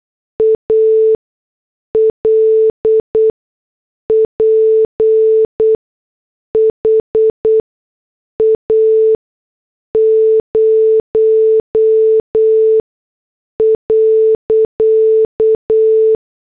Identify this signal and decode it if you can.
Morse code, heard as a keyed tone, about 8 wpm, spaced as for 6 wpm, 440 Hz. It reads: ALPHA 0.
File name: morse / headlines